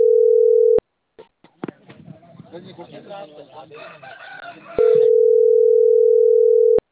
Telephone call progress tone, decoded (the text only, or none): ringback tone